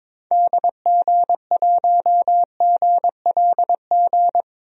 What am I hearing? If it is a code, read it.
Morse code, heard as a keyed tone, about 22 wpm, 700 Hz. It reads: DG1GLG
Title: DXCC-100 Call Signs - 6 Characters - Repeated 2x - 22wpm